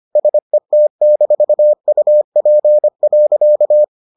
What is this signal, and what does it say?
Morse code, heard as a keyed tone, about 25 wpm, 600 Hz. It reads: SET-UP.